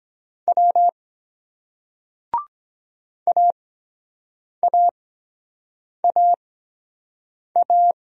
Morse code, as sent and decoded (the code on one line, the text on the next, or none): .-- / .- .- .- .-
W AAAA